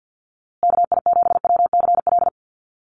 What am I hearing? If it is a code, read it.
Morse code, heard as a keyed tone, about 50 wpm, 700 Hz. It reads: KI7WCL
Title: USA Call Signs Repeated 3x 50wpm